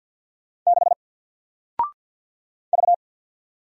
Morse code, heard as a keyed tone, about 50 wpm, 700 Hz. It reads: 6 V
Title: Single Character (Letter:Number:Punctuation:Prosign) 50wpm